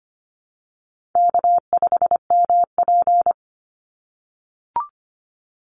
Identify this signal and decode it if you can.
Morse code, heard as a keyed tone, about 25 wpm, 700 Hz. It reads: K5MP
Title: Easy Call Signs - 25wpm